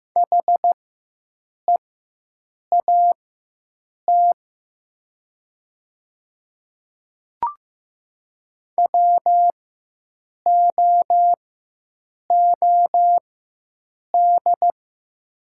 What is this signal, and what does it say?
Morse code, heard as a keyed tone, about 15 wpm, spaced as for 7 wpm, 700 Hz. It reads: HEAT WOOD
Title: 4x Character Spacing - Top 500 Words - 15wpm